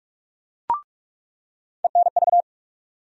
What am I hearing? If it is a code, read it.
Morse code, heard as a keyed tone, about 46 wpm, 700 Hz. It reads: ENV
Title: USA Call Sign Postfixes - 3 Letters 45wpm